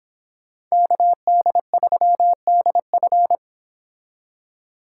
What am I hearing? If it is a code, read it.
Morse code, heard as a keyed tone, about 30 wpm, 700 Hz. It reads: KD3DF